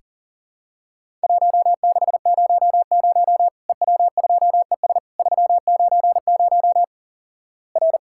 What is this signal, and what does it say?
Morse code, heard as a keyed tone, about 38 wpm, 700 Hz. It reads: 1600 EW2ES 390 R
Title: Worked All Europe DX Contest - QTC Exchange - 40wpm